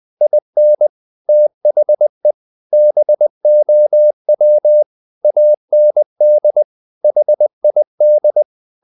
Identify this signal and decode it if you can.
Morse code, heard as a keyed tone, about 20 wpm, 600 Hz. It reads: IN THE BOW AND HID
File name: the old man and the sea part 2 20wpm